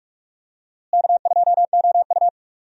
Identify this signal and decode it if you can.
Morse code, heard as a keyed tone, about 46 wpm, 700 Hz. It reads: K2OU
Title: USA Call Signs - 4 Characters - Repeated 3x - 45wpm